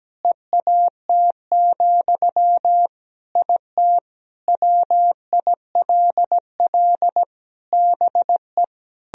Morse code, heard as a keyed tone, about 17 wpm, 700 Hz. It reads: EAT, IT WILL BE
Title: Sentences from Top 1000 Words - Review Part 1 - 17wpm